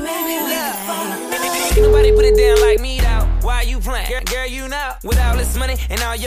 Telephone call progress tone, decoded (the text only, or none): ringback tone